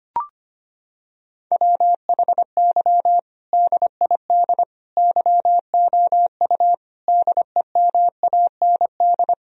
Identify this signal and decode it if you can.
Morse code, heard as a keyed tone, about 25 wpm, 700 Hz. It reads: WHY DID YOU DEMAND
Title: Sentences from Top 1000 Words - Review Part 4 - 25wpm